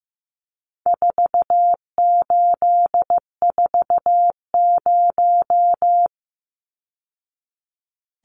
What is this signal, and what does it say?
Morse code, heard as a keyed tone, about 15 wpm, 700 Hz. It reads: 4840
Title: Numbers - 4-Digit 15wpm